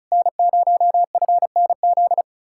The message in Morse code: N0FNZ

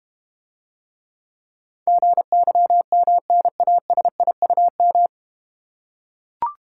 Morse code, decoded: GYMNASIUM

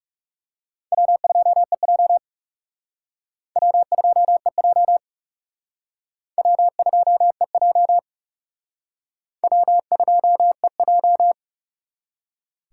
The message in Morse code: W2EJ W2EJ W2EJ W2EJ

